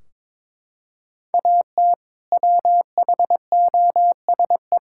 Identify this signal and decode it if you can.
Morse code, heard as a keyed tone, about 22 wpm, 700 Hz. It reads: AT WHOSE
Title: Sentences from Top 1600 Words 22wpm